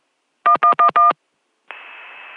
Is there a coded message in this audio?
Yes: DTMF from 0.5 to 1.1 s, heard as telephone touch tones, tones of about 100 ms. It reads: 1111